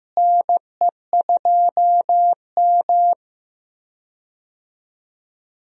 Morse code, heard as a keyed tone, about 16 wpm, 700 Hz. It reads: NE2M
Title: USA Call Signs - 4 Characters - Repeated 2x - 15wpm